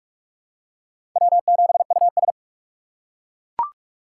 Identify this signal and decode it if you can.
Morse code, heard as a keyed tone, about 42 wpm, 700 Hz. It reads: W7US
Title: USA Call Signs - 3x Character Spacing - 45wpm